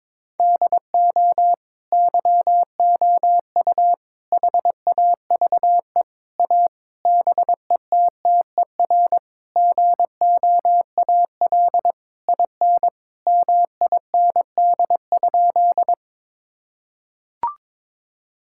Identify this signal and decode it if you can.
Morse code, heard as a keyed tone, about 22 wpm, 700 Hz. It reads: DO YOU HAVE A BETTER GOAL IN MIND?
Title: Sentences from Top 1000 Words - Review Part 5 - 22wpm